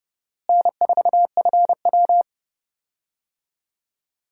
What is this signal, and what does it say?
Morse code, heard as a keyed tone, about 32 wpm, 700 Hz. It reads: N4FW
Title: USA Call Signs - 4 Characters - Repeated 3x - 30wpm